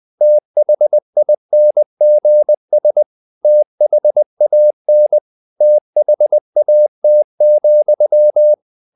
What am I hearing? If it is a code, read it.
Morse code, heard as a keyed tone, about 20 wpm, 600 Hz. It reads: THINGS THAN THAT,
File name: the old man and the sea part 2 20wpm